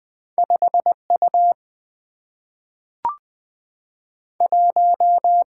Morse code, heard as a keyed tone, about 20 wpm, 700 Hz. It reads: 5U 1
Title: Single-Letter Single-Number 20wpm